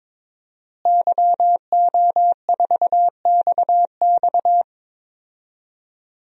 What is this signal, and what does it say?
Morse code, heard as a keyed tone, about 22 wpm, 700 Hz. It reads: YO4XX
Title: DXCC-100 Call Signs - 5 Characters - Repeated 3x - 22wpm